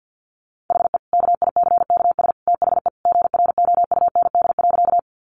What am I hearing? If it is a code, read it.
Morse code, heard as a keyed tone, about 50 wpm, 700 Hz. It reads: HE KICKS THE GROUND.
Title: Additional Easy Phrases 50wpm